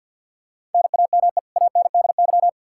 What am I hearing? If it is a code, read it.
Morse code, heard as a keyed tone, about 50 wpm, 700 Hz. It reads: NAME ANDY
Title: Practice QSO - Exchange 1 - Beginning - 50wpm